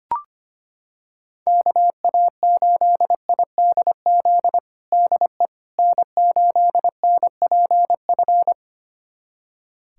Morse code, heard as a keyed tone, about 25 wpm, 700 Hz. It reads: KA8IDZ DE N8NPF
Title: USA Callsign DE Callsign 25wpm